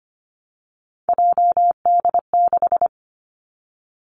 Morse code, heard as a keyed tone, about 25 wpm, 700 Hz. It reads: JD6
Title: Three Challenging Letter-Number Combinations 25wpm